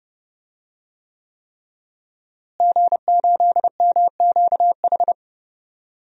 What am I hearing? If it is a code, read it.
Morse code, heard as a keyed tone, about 30 wpm, 700 Hz. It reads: G8MQH
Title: DXCC-100 Call Signs - 5 Characters - Repeated 3x - 30wpm